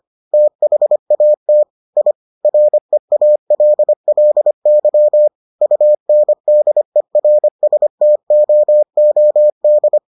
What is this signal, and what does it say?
Morse code, heard as a keyed tone, about 25 wpm, 600 Hz. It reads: THAT I REALLY UNDERSTOOD